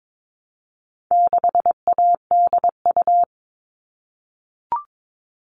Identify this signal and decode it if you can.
Morse code, heard as a keyed tone, about 22 wpm, 700 Hz. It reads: 6ADU
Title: All Cut Numbers - 4-Digit 22wpm